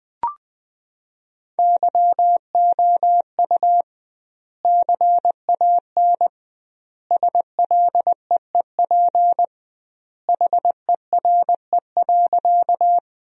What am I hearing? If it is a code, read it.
Morse code, heard as a keyed tone, about 20 wpm, 700 Hz. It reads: YOU CAN SLEEP HERE.
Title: 2x Word-Spacing - Sentences from Top 500 Words - 20wpm